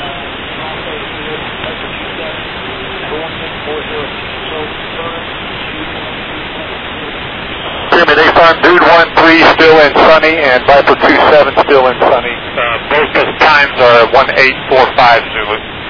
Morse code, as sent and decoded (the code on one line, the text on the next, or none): none